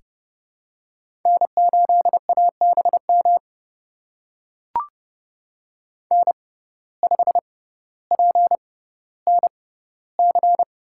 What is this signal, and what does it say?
Morse code, heard as a keyed tone, about 30 wpm, spaced as for 15 wpm, 700 Hz. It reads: N8ABM N5PNC
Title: USA Call Signs - 5 Characters - 6x Character Spacing - 30wpm